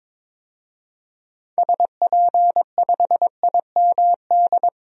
Morse code, heard as a keyed tone, about 21 wpm, 700 Hz. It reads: SP5IMD
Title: DXCC-100 Call Signs - 6 Characters - 22wpm